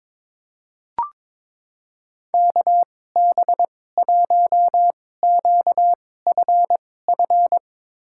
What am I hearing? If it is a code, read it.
Morse code, heard as a keyed tone, about 22 wpm, spaced as for 16 wpm, 700 Hz. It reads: KB1QFF
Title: USA Call Signs - 6 Characters - 2x Character Spacing - 22wpm